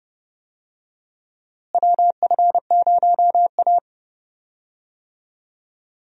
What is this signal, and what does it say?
Morse code, heard as a keyed tone, about 30 wpm, 700 Hz. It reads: WF0A
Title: USA Call Signs - 4 Characters - Repeated 2x - 30wpm